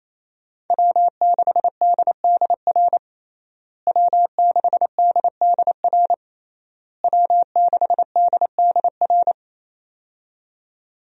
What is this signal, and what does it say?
Morse code, heard as a keyed tone, about 28 wpm, 700 Hz. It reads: W6DDR W6DDR W6DDR
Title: USA Call Signs - 5 Characters - Repeated 3x - 28wpm